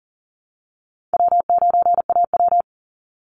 Morse code, heard as a keyed tone, about 40 wpm, 700 Hz. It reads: W9AW
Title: Easy Call Signs Repeated 3x - 40wpm